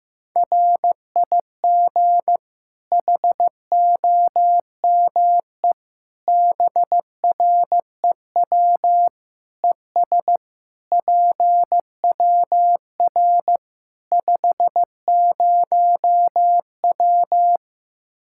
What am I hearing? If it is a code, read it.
Morse code, heard as a keyed tone, about 15 wpm, 700 Hz. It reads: RIG HOME BREW ES PWR 50W
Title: K1USN SST QSO Practice 15wpm